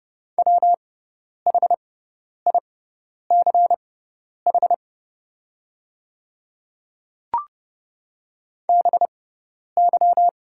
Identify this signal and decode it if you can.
Morse code, heard as a keyed tone, about 30 wpm, spaced as for 10 wpm, 700 Hz. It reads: WHICH BY
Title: 6x Character Spacing - Top 100 Words - 30wpm